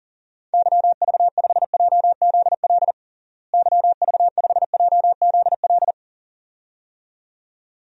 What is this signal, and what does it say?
Morse code, heard as a keyed tone, about 37 wpm, 700 Hz. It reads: YV5JZL YV5JZL
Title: DXCC-100 Call Signs - 6 Characters - Repeated 2x - 40wpm